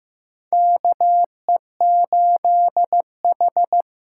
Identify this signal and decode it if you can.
Morse code, heard as a keyed tone, about 15 wpm, 700 Hz. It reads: KE8H
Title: USA Call Signs - 6 Characters - 6x Character Spacing - 15wpm